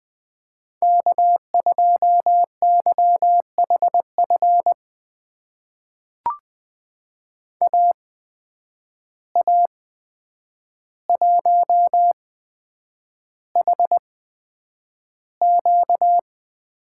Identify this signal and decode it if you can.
Morse code, heard as a keyed tone, about 20 wpm, spaced as for 9 wpm, 700 Hz. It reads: K2YHF AA1HQ
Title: USA Call Signs - 5 Characters - 8x Character Spacing - 20wpm